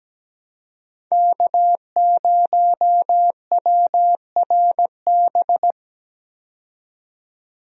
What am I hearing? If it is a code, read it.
Morse code, heard as a keyed tone, about 17 wpm, 700 Hz. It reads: K0WRB
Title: USA Call Signs - 5 Characters - 17wpm